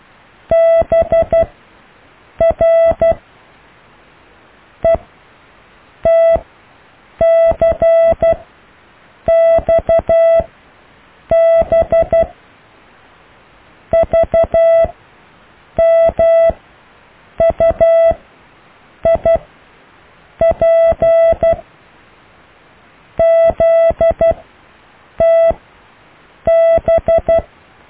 Morse code, heard as a keyed tone, about 12 wpm, spaced as for 6 wpm, 660 Hz. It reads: BRETC XBVMU IPZTB
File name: cw / Alle letters